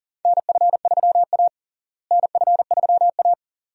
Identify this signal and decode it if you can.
Morse code, heard as a keyed tone, about 40 wpm, 700 Hz. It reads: NF3A NF3A
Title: USA Call Signs - 4 Characters - Repeated 2x - 40wpm